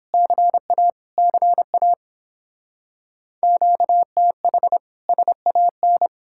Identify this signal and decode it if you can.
Morse code, heard as a keyed tone, about 29 wpm, 700 Hz. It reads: CA CA QTH SAN